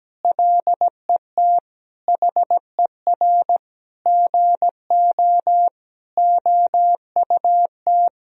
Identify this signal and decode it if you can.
Morse code, heard as a keyed tone, about 17 wpm, 700 Hz. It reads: LET HER GO OUT
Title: Sentences from Top 1000 Words - Review Part 2 - 17wpm